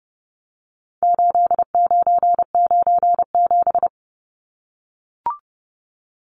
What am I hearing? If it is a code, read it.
Morse code, heard as a keyed tone, about 30 wpm, 700 Hz. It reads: 8997
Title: Numbers - 4-Digit 30wpm